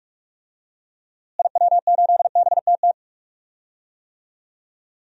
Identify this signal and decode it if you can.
Morse code, heard as a keyed tone, about 41 wpm, 700 Hz. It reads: IW8BTT
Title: DXCC-100 Call Signs - 6 Characters - Repeated 3x - 45wpm